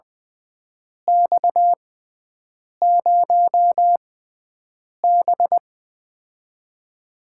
Morse code, heard as a keyed tone, about 20 wpm, spaced as for 7 wpm, 700 Hz. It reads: X0B
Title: USA Call Signs - 6x Character Spacing - 20wpm